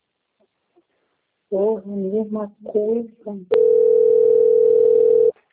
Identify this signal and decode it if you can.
Ringback tone (North America), 440 + 480 Hz, with speech.
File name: out- - - - - .31275